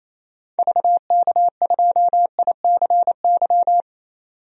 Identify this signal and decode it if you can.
Morse code, heard as a keyed tone, about 28 wpm, 700 Hz. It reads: VK2ICY
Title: DXCC-100 Call Signs - 6 Characters - 28wpm